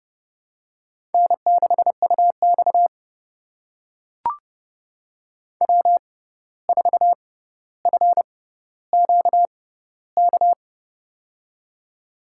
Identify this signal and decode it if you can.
Morse code, heard as a keyed tone, about 30 wpm, spaced as for 14 wpm, 700 Hz. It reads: N6UX W4FQK
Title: USA Call Signs - 6x Character Spacing - 30wpm